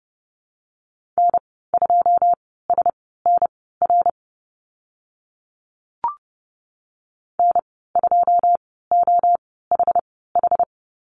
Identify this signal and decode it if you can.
Morse code, heard as a keyed tone, about 30 wpm, spaced as for 17 wpm, 700 Hz. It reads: N2SNR N2OHH
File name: Lesson 15g - Number 5 - Call Signs 30wpm